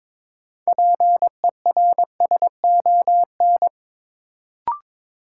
Morse code, heard as a keyed tone, about 22 wpm, 700 Hz. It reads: PERSON
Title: Sentences from Top 400 Words - New Word Drill - 22wpm